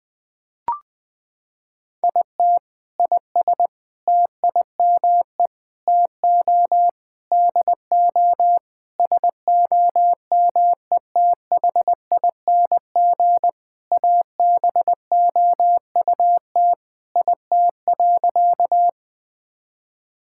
Morse code, heard as a keyed tone, about 20 wpm, 700 Hz. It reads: IT IS TIME TO DO SOMETHING ABOUT IT.
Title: Sentences with 5-Word Phrases - 20wpm